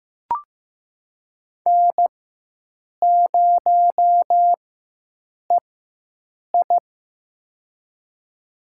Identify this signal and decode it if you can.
Morse code, heard as a keyed tone, about 15 wpm, spaced as for 7 wpm, 700 Hz. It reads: N0EI
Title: USA Call Signs - 4x Character Spacing - 15wpm